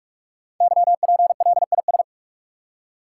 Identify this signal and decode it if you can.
Morse code, heard as a keyed tone, about 44 wpm, 700 Hz. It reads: YPLIS
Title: Code Groups - 5 Letters - 45wpm